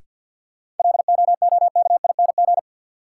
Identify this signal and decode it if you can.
Morse code, heard as a keyed tone, about 47 wpm, 700 Hz. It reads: LOOKING
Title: Sentences from Top 1000 Words - Review Part 6 - New Word Drill - 50wpm